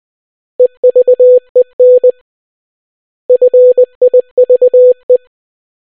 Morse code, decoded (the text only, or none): EVEN FIVE